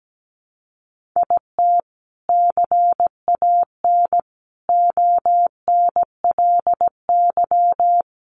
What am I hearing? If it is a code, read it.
Morse code, heard as a keyed tone, about 17 wpm, 700 Hz. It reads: IT CAN ONLY
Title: Sentences from Top 1400 Words 17wpm